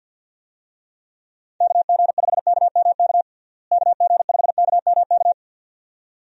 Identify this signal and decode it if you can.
Morse code, heard as a keyed tone, about 53 wpm, 700 Hz. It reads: KG5KMK KG5KMK
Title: USA Call Signs - 6 Characters - Repeated 2x - 50wpm